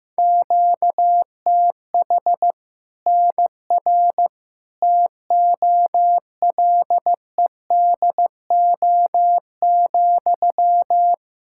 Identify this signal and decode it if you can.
Morse code, heard as a keyed tone, about 15 wpm, 700 Hz. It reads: QTH NR TOLEDO,
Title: Practice QSO - Exchange 1 - Beginning - 15wpm